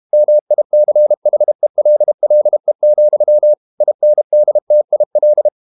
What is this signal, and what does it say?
Morse code, heard as a keyed tone, about 32 wpm, 600 Hz. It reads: MICHELLE, INDTIL
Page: Ferie 32 wpm